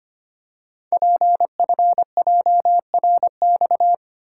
Morse code, heard as a keyed tone, about 25 wpm, 690 Hz. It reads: PFJRX
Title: Code Groups - 5 Letters - 25wpm